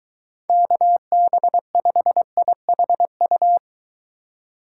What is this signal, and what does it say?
Morse code, heard as a keyed tone, about 23 wpm, 700 Hz. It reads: KB5IHU